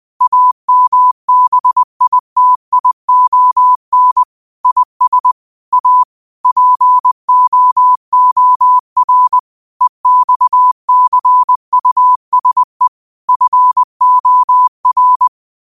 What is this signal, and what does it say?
Morse code, heard as a keyed tone, about 20 wpm, 1000 Hz. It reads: AMBITION IS A POOR EXCUSE FOR
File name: qotdfiles / 20wpm